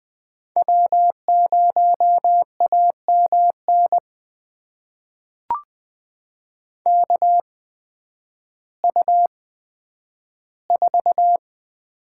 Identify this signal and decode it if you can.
Morse code, heard as a keyed tone, about 20 wpm, spaced as for 11 wpm, 700 Hz. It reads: W0AMN KU4
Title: USA Call Signs - 5 Characters - 8x Character Spacing - 20wpm